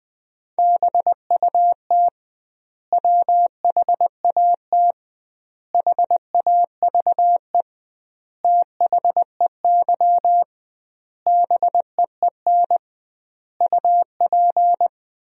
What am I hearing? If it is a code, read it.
Morse code, heard as a keyed tone, about 21 wpm, 700 Hz. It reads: BUT WHAT HAVE THEY BEEN UP
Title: 2x Word-Spacing - Sentences from Top 100 Words - 20wpm